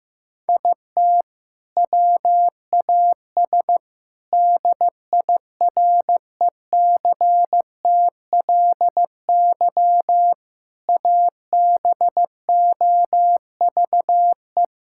Morse code, heard as a keyed tone, about 15 wpm, 700 Hz. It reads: IT WAS DIRECTLY ABOVE
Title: Sentences from Top 1000 Words - Review Part 5 - 15wpm